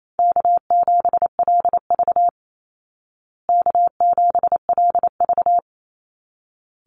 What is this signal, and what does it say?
Morse code, heard as a keyed tone, about 28 wpm, 700 Hz. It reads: K7LV K7LV
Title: Easy Call Signs Repeated 2x - 28wpm